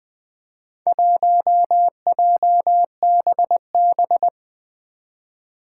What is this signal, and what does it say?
Morse code, heard as a keyed tone, about 21 wpm, 700 Hz. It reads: 1JBB
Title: Four Challenging Letter-Number Combinations 20wpm